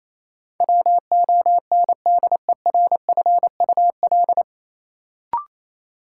Morse code, heard as a keyed tone, about 28 wpm, 700 Hz. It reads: WONDERFUL
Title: Sentences from Top 1100 Words - New Word Drill Alt - 28wpm